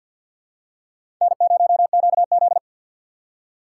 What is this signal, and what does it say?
Morse code, heard as a keyed tone, about 51 wpm, 700 Hz. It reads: N0QZ
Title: USA Call Signs - 4 Characters - 6x Character Spacing - 50wpm